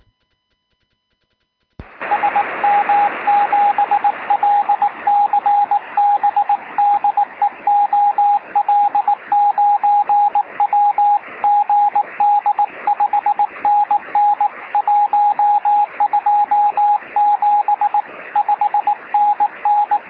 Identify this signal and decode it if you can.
Morse code, heard as a keyed tone, about 19 wpm, 830 Hz, with radio noise. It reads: SM7LCB DE OL9W GD 5NN 127 5NN
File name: sm7lcb / 0802z ol9w 23cm